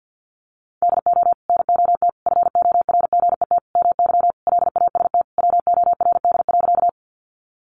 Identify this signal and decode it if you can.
Morse code, heard as a keyed tone, about 50 wpm, 700 Hz. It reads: DO NOT FORGET MY LAST WORD.
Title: Sentences from Top 1200 Words 50wpm